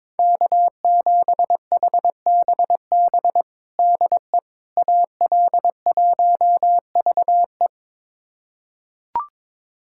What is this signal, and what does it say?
Morse code, heard as a keyed tone, about 22 wpm, 700 Hz. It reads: K7HBB DE AL1VE
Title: USA Callsign DE Callsign - 5 Characters - 22wpm